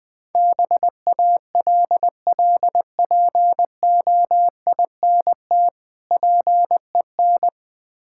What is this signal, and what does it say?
Morse code, heard as a keyed tone, about 21 wpm, 700 Hz. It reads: BALLPOINT PEN
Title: Easy Words - Speed-Racing with Hints - 20wpm